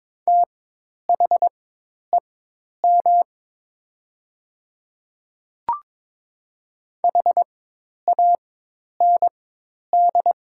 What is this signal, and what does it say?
Morse code, heard as a keyed tone, about 22 wpm, spaced as for 10 wpm, 690 Hz. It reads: THEM HAND
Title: 4x Character Spacing - Top 500 Words - 22wpm